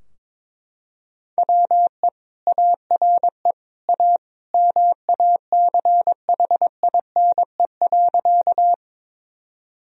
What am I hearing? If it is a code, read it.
Morse code, heard as a keyed tone, about 22 wpm, 700 Hz. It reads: WE ARE A MACHINE.
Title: Sentences from Top 1000 Words - Review Part 2 - 22wpm